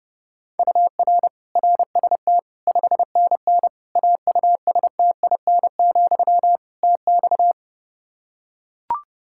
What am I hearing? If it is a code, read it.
Morse code, heard as a keyed tone, about 30 wpm, 700 Hz. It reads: UR RST 5NN AUSTIN, TX
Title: Practice QSO - Exchange 1 - Beginning - 30wpm